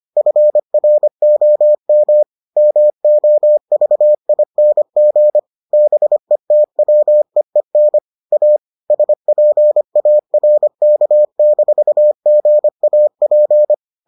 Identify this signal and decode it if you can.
Morse code, heard as a keyed tone, about 25 wpm, 600 Hz. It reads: FROM MOVING BETWEEN A SPARK-GAP